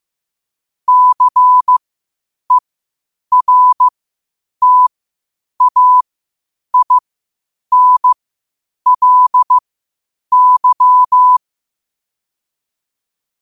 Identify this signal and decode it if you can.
Morse code, heard as a keyed tone, about 15 wpm, spaced as for 8 wpm, 1000 Hz. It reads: CERTAINLY
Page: New quotes every day in morse code at 8 Words per minute.